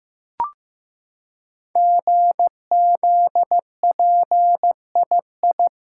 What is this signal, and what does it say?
Morse code, heard as a keyed tone, about 15 wpm, 700 Hz. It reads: GZPII
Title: Code Groups - 5 Letter:Numbers - 15wpm